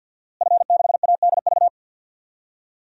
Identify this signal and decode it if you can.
Morse code, heard as a keyed tone, about 51 wpm, 700 Hz. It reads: F6ADV
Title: DXCC-100 Call Signs - 5 Characters - Repeated 3x - 50wpm